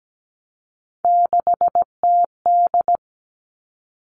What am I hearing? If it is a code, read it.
Morse code, heard as a keyed tone, about 17 wpm, 700 Hz. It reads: 6TD
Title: All Cut Numbers - 3-Digit 17wpm